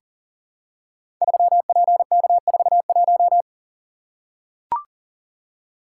Morse code, heard as a keyed tone, about 39 wpm, 700 Hz. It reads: 3PK41
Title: Code Groups - 5 Letter:Numbers - 40wpm